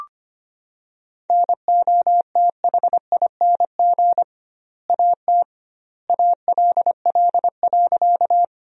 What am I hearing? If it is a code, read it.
Morse code, heard as a keyed tone, about 25 wpm, 700 Hz. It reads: NOTHING AT ALL.
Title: 2x Word-Spacing - Sentences from Top 500 Words - 25wpm